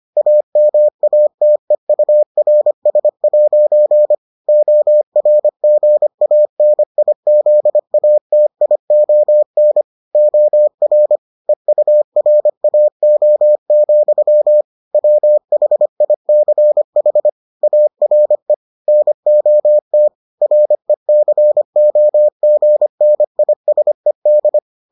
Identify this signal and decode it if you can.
Morse code, heard as a keyed tone, about 25 wpm, 600 Hz. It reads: AMATEURS' ORGANIZATION OR EURAO, WHICH ARE NOT RECOGNISED